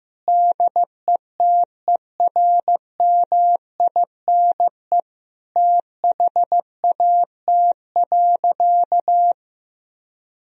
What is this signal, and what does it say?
Morse code, heard as a keyed tone, about 15 wpm, 700 Hz. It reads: DETERMINE THAT.
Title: Sentences from Top 1000 Words - Review Part 4 - 15wpm